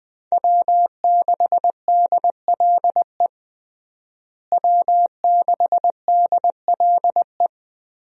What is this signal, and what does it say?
Morse code, heard as a keyed tone, about 21 wpm, 700 Hz. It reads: W6DLE W6DLE
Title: USA Call Signs - 5 Characters - Repeated 3x - 20wpm